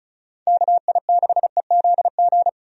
Morse code, heard as a keyed tone, about 36 wpm, 700 Hz. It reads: KI6EZG
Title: USA Call Signs - 6 Characters - 35wpm